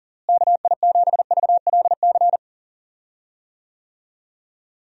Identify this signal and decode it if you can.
Morse code, heard as a keyed tone, about 40 wpm, 700 Hz. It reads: KI7VLC
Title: USA Call Signs - 6 Characters - Repeated 3x - 40wpm